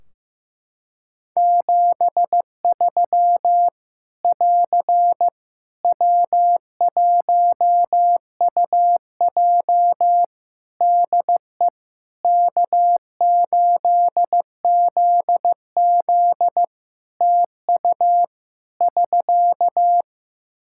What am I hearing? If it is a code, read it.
Morse code, heard as a keyed tone, about 15 wpm, 700 Hz. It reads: 73 <AR> W1UJ DE K8ZZ TU <SK>
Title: Practice QSO - Exchange 4 - Ending - 15wpm